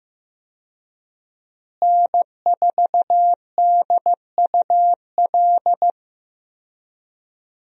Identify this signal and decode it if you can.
Morse code, heard as a keyed tone, about 15 wpm, 700 Hz. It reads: N4DUL
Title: USA Call Signs - 5 Characters - Repeated 3x - 15wpm